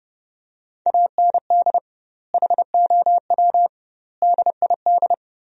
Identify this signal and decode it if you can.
Morse code, heard as a keyed tone, about 32 wpm, 700 Hz. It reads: AND HOW DID
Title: 2x Word-Spacing - Sentences from Top 100 Words - 30wpm